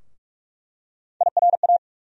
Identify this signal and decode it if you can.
Morse code, heard as a keyed tone, about 47 wpm, 700 Hz. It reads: IRA
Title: USA Call Sign Postfixes - 3 Letters 45wpm